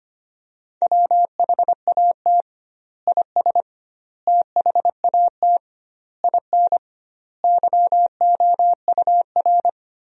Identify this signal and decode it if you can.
Morse code, heard as a keyed tone, about 25 wpm, 700 Hz. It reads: WHAT IS THAT IN YOUR
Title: 2x Word-Spacing - Sentences from Top 200 Words - 25wpm